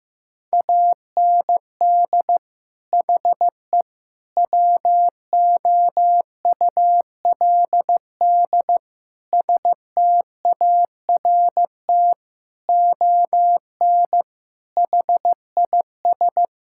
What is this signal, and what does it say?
Morse code, heard as a keyed tone, about 15 wpm, 700 Hz. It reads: AND HE WOULD START ON HIS
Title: Sentences from Top 1600 Words 15wpm